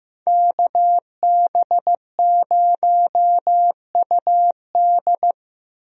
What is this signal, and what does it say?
Morse code, heard as a keyed tone, about 15 wpm, 700 Hz. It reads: KB0UD
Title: USA Call Signs - 5 Characters - Repeated 3x - 15wpm